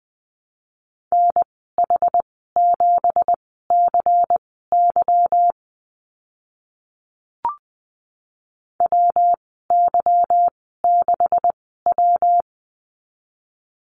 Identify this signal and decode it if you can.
Morse code, heard as a keyed tone, about 19 wpm, spaced as for 14 wpm, 700 Hz. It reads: NH7CY WY6W
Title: 167 Lesson 27g - Number 7 - Call Signs 20wpm